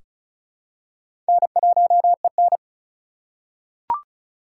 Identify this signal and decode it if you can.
Morse code, heard as a keyed tone, about 34 wpm, 700 Hz. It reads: N1EN
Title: K1USN SST QSO Practice - 35wpm